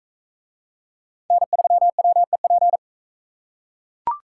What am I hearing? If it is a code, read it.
Morse code, heard as a keyed tone, about 40 wpm, 700 Hz. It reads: N3WEP